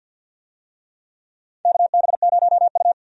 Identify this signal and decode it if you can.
Morse code, heard as a keyed tone, about 52 wpm, 700 Hz. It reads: KB0U
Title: USA Call Signs - 4 Characters - 50wpm